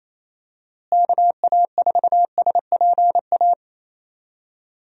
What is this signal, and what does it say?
Morse code, heard as a keyed tone, about 28 wpm, 700 Hz. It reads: KA4SPA
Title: USA Call Signs - 6 Characters - Repeated 2x - 28wpm